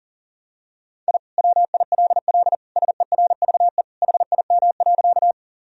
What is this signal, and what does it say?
Morse code, heard as a keyed tone, about 37 wpm, 700 Hz. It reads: I WILL SERVE HIM.
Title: Sentences from Top 1000 Words - Review Part 2 - 40wpm